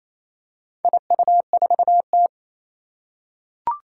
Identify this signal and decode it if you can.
Morse code, heard as a keyed tone, about 27 wpm, 700 Hz. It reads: IU4T
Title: Four Challenging Letter-Number Combinations 28wpm